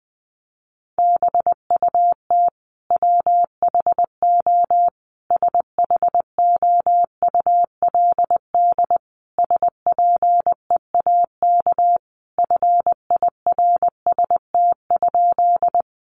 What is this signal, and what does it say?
Morse code, heard as a keyed tone, about 20 wpm, 700 Hz. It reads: BUT WHO SHOULD SPEAK FIRST?
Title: Sentences from Top 1100 Words 20wpm